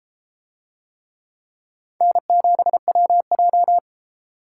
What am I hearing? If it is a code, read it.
Morse code, heard as a keyed tone, about 32 wpm, 700 Hz. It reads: N7WJ